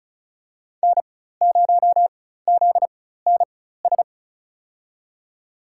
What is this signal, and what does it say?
Morse code, heard as a keyed tone, about 36 wpm, spaced as for 16 wpm, 700 Hz. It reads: N0ZNS
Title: USA Call Signs - 4x Character Spacing - 35wpm